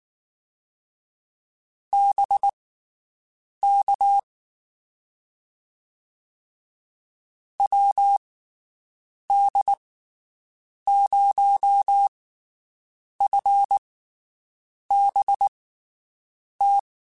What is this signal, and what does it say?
Morse code, heard as a keyed tone, about 19 wpm, spaced as for 7 wpm, 800 Hz. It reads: BK WD0FBT
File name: QSOs Spaced